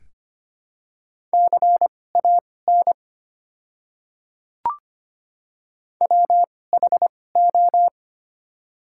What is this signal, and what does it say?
Morse code, heard as a keyed tone, about 25 wpm, spaced as for 18 wpm, 700 Hz. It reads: CAN WHO
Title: 2x Character Spacing - Top 100 Words - 25wpm